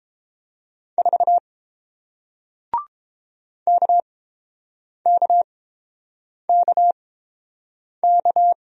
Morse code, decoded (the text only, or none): 4 KKKK